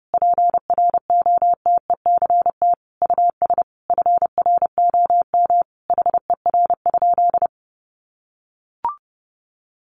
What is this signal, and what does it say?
Morse code, heard as a keyed tone, about 30 wpm, 700 Hz. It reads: PROTECT US FROM HER?
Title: Sentences from Top 900 Words 30wpm